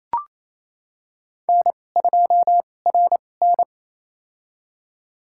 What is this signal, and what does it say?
Morse code, heard as a keyed tone, about 28 wpm, spaced as for 20 wpm, 700 Hz. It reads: N2RN
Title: USA Call Signs - 4 Characters - 2x Character Spacing - 28wpm